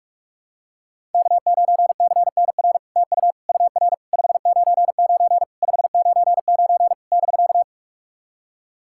Morse code, heard as a keyed tone, about 44 wpm, 700 Hz. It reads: K9CNR TU UR 599 599 <BK>
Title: SOTA QSO Practice 45wpm